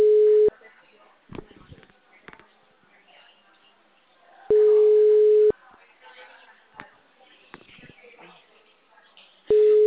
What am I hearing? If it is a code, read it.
Ringback tone (Europe, 425 Hz), with speech.